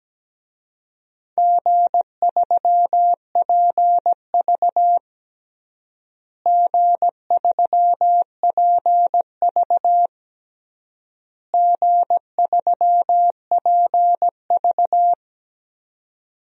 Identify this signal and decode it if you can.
Morse code, heard as a keyed tone, about 17 wpm, 700 Hz. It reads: G3PV G3PV G3PV
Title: DXCC-100 Call Signs - 4 Characters - Repeated 3x - 17wpm